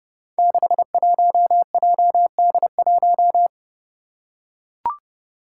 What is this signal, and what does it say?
Morse code, heard as a keyed tone, about 31 wpm, 700 Hz. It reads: 61JD1
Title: Five Challenging Letter-Number Combinations 30wpm